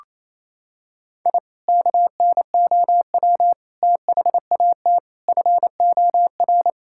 Morse code, heard as a keyed tone, about 28 wpm, 700 Hz. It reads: I KNOW THAT FOR
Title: Sentences from Top 1000 Words - Review Part 5 - 28wpm